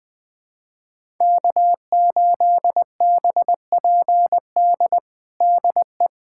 Morse code, heard as a keyed tone, about 20 wpm, 700 Hz. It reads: K8BPD DE
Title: USA Callsign DE Callsign - 5 Characters - 20wpm